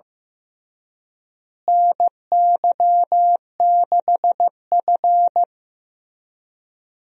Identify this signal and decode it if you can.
Morse code, heard as a keyed tone, about 15 wpm, 700 Hz. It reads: NY6F
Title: USA Call Signs - 4 Characters - Repeated 3x - 15wpm